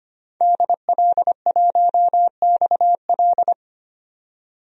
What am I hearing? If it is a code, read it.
Morse code, heard as a keyed tone, about 25 wpm, 690 Hz. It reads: DL1XL
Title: DXCC-100 Call Signs - 5 Characters - 25wpm